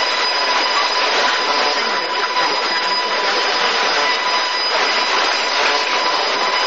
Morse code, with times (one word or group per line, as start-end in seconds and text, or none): none